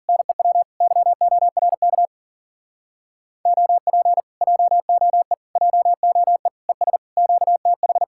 Morse code, heard as a keyed tone, about 52 wpm, 700 Hz. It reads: NEW YORK OP JOE JOE ES QTH